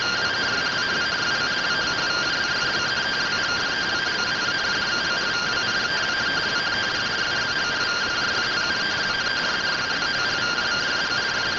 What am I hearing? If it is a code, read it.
Two-tone teleprinter signal, about 170 Hz shift.